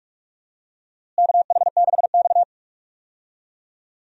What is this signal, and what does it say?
Morse code, heard as a keyed tone, about 43 wpm, 700 Hz. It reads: KH6X
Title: DXCC-100 Call Signs - 4 Characters - Repeated 2x - 45wpm